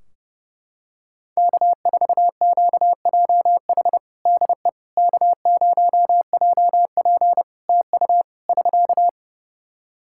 Morse code, heard as a keyed tone, about 30 wpm, 700 Hz. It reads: K4QJH DE K0JP TU <SK>
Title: Practice QSO - Exchange 4 - Ending - 30wpm